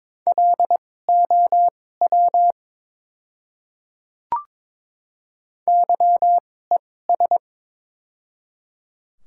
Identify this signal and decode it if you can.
Morse code, heard as a keyed tone, about 22 wpm, spaced as for 16 wpm, 700 Hz. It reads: LOW YES
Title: 2x Character Spacing - Top 500 Words - 22wpm